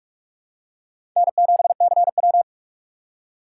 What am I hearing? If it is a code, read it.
Morse code, heard as a keyed tone, about 46 wpm, 700 Hz. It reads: N7CW
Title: K1USN SST QSO Practice - 45wpm